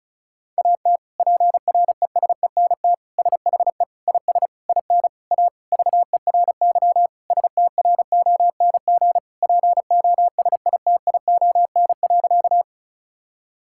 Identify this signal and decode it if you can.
Morse code, heard as a keyed tone, about 36 wpm, 700 Hz. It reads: AT PRESENT SHE IS IN A VERY STRONG POSITION.
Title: Sentences from Top 1000 Words - Review Part 5 - 35wpm